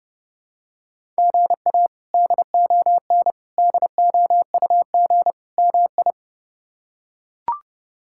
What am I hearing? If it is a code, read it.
Morse code, heard as a keyed tone, about 30 wpm, 700 Hz. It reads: GA DON DOUG MI
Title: K1USN SST QSO Practice - 30wpm